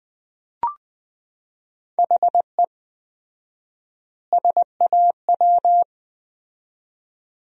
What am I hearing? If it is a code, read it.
Morse code, heard as a keyed tone, about 20 wpm, 700 Hz. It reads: HE SAW
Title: 4x Word-Spacing - Sentences from Top 500 Words - 20wpm